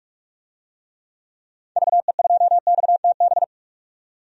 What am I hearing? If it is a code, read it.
Morse code, heard as a keyed tone, about 43 wpm, 700 Hz. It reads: VE2XTB
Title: DXCC-100 Call Signs - 6 Characters - 45wpm